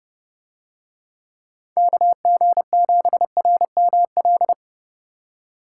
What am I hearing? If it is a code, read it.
Morse code, heard as a keyed tone, about 29 wpm, 700 Hz. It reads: KG7RML